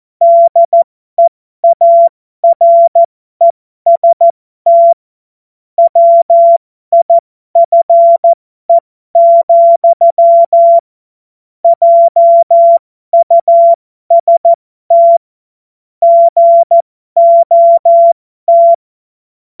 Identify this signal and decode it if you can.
Morse code, heard as a keyed tone, about 14 wpm, spaced as for 12 wpm, 680 Hz. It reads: DEAREST WIFE, JUST GOT
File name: Morsekurs / Witze